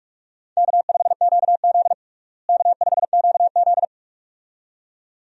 Morse code, heard as a keyed tone, about 41 wpm, 700 Hz. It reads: K5QZ K5QZ
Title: USA Call Signs - 4 Characters - Repeated 2x - 45wpm